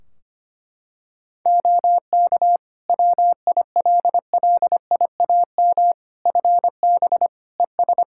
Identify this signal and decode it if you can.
Morse code, heard as a keyed tone, about 25 wpm, 700 Hz. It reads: OK WILLIAM FB ES
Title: Practice QSO - Exchange 2 - Info - 25wpm